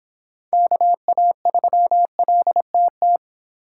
Morse code, heard as a keyed tone, about 26 wpm, 700 Hz. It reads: KA3LTT